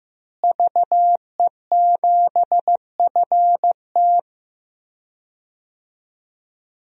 Morse code, heard as a keyed tone, about 15 wpm, 700 Hz. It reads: VE7FT
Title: DXCC-100 Call Signs - 5 Characters - Repeated 2x - 15wpm